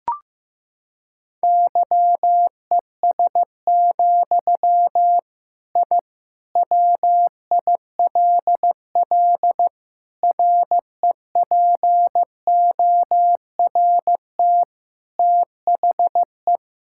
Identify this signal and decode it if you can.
Morse code, heard as a keyed tone, about 16 wpm, 700 Hz. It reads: YES, I WILL REPORT THE
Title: Sentences from Top 1000 Words - Review Part 6 - 15wpm